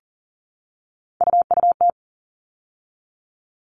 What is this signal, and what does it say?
Morse code, heard as a keyed tone, about 40 wpm, 700 Hz. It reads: UUT
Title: All Cut Numbers - 3-Digit 40wpm